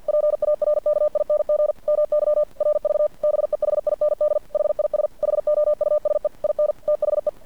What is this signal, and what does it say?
Morse code, heard as a keyed tone, about 50 wpm, 600 Hz. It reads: PARKING MY RV BEHIND HIS HOUSE IN THE
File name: archived / CRSnet